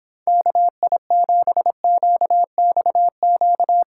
Morse code, heard as a keyed tone, about 26 wpm, 700 Hz. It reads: KI7QXQ